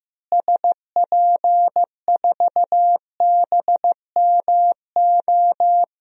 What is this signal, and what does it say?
Morse code, heard as a keyed tone, about 15 wpm, 700 Hz. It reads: SP4BMO
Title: DXCC-100 Call Signs - 6 Characters - Repeated 3x - 15wpm